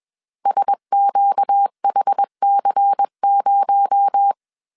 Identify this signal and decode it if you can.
Morse code, heard as a keyed tone, about 21 wpm, 800 Hz. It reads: SQHC0
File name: CW / DET / 1 6 1500C S A 20wpm 2048